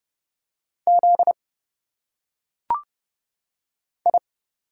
Morse code, heard as a keyed tone, about 32 wpm, 700 Hz. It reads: Z I
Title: Single Letters - Phonetic Pronunciation 30wpm